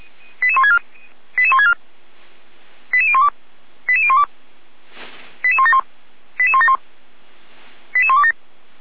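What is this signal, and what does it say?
Five-tone selective call (ZVEI), heard as quick runs of five tones, tones of about 70 ms. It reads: 88164 88164 88012 88012 88171 88171 88127
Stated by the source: ZVEI five-tone selective call received with an SDR.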